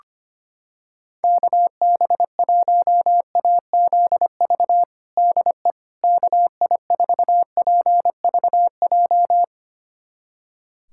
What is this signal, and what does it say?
Morse code, heard as a keyed tone, about 25 wpm, 700 Hz. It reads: KB1AZV DE KI4PVJ
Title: USA Callsign DE Callsign - 6 Characters - 25wpm